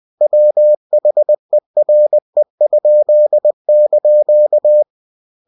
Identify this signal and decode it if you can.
Morse code, heard as a keyed tone, about 20 wpm, 600 Hz. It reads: WHERE?)